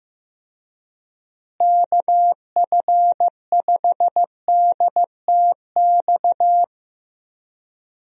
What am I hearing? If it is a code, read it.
Morse code, heard as a keyed tone, about 15 wpm, 700 Hz. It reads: KF5DTX
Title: USA Call Signs - 6 Characters - Repeated 3x - 15wpm